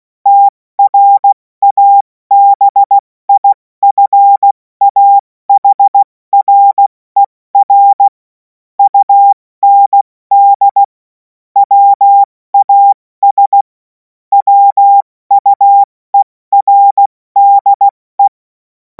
Morse code, heard as a keyed tone, about 16 wpm, spaced as for 14 wpm, 800 Hz. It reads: TRABIFAHRER UND WAS WUERDE
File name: Morsekurs / Witze / Witz 21. April 2026 70 F80